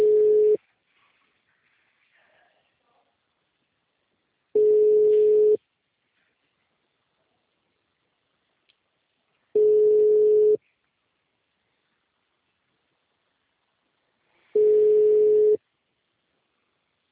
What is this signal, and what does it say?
Ringback tone (Europe, 425 Hz).